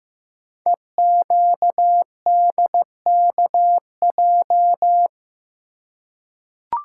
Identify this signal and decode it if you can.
Morse code, heard as a keyed tone, about 15 wpm, 700 Hz. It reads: EQDKJ
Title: Code Groups - 5 Letters - 15wpm